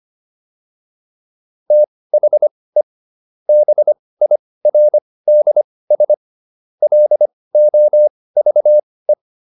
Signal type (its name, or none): Morse code